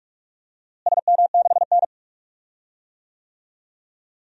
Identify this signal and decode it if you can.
Morse code, heard as a keyed tone, about 46 wpm, 700 Hz. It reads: SM6N
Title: DXCC-100 Call Signs - 4 Characters - Repeated 3x - 45wpm